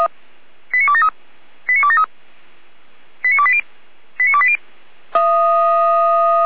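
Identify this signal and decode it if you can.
Five-tone selective call (ZVEI), heard as quick runs of five tones, tones of about 70 ms. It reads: 89273 89273 89280 89280
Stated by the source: ZVEI five-tone call received by SDR, from audio.f85360.feuerwehr-orlamuende.org (Feuerwehr Orlamünde).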